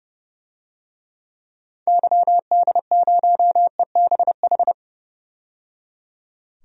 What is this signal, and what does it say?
Morse code, heard as a keyed tone, about 30 wpm, 700 Hz. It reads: YD0EBH
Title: DXCC-100 Call Signs - 6 Characters - 30wpm